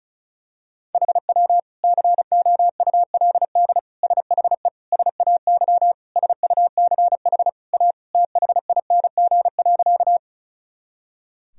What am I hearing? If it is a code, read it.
Morse code, heard as a keyed tone, about 34 wpm, 700 Hz. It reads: HW COULD SHE SAY SUCH A THING.
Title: Sentences from Top 200 Words and Abbreviations 35wpm